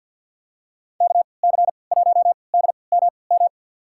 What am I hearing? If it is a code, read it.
Morse code, heard as a keyed tone, about 50 wpm, spaced as for 28 wpm, 700 Hz. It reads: KC1DMM
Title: USA Call Signs - 6 Characters - 3x Character Spacing - 50wpm